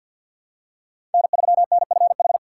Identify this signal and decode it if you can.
Morse code, heard as a keyed tone, about 53 wpm, 700 Hz. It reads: N3NFH
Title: USA Call Signs - 5 Characters - 2x Character Spacing - 50wpm